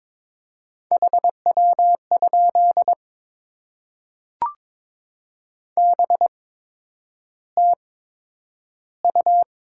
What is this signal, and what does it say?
Morse code, heard as a keyed tone, about 22 wpm, spaced as for 9 wpm, 700 Hz. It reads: HW? BTU
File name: Top 20 QSO Elements - 8x Character Spacing - 22wpm